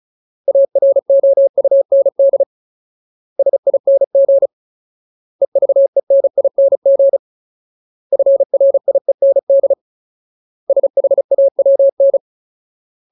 Morse code, heard as a keyed tone, about 34 wpm, 550 Hz. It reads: AROUND SING EVENING FRIEND SHAWN
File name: Ultimate Word List / 35WPM Ultimate